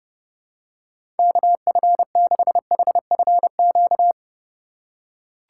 Morse code, heard as a keyed tone, about 30 wpm, 700 Hz. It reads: KF6HFQ
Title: USA Call Signs - 6 Characters - 4x Character Spacing - 30wpm